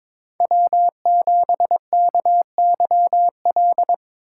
Morse code, heard as a keyed tone, about 26 wpm, 700 Hz. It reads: W7KYL
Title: Speed Racing - USA 1x3 Call Signs 22wpm